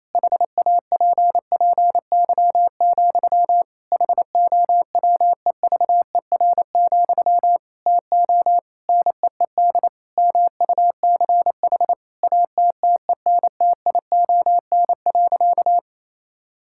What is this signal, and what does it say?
Morse code, heard as a keyed tone, about 28 wpm, 700 Hz. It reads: HAPPY, HOWEVER, TO NEED MUCH ATTENTION.
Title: Sentences from Top 1000 Words - Review Part 6 - 28wpm